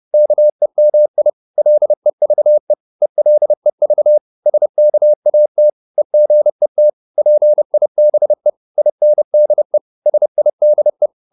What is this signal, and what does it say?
Morse code, heard as a keyed tone, about 30 wpm, 600 Hz. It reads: KEMI LEVE ELEV SKAT EGET PIBE INDE SIDE
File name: Korte ord DK 30wpm